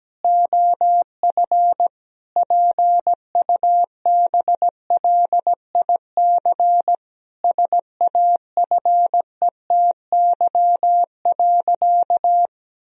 Morse code, heard as a keyed tone, about 17 wpm, 700 Hz. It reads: OF PUBLIC SAFETY.
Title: Sentences from Top 1000 Words - Review Part 4 - 17wpm